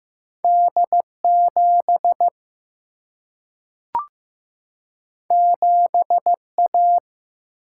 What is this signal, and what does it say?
Morse code, heard as a keyed tone, about 15 wpm, 700 Hz. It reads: D 7 7 A
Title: Single-Letter Single-Number 15wpm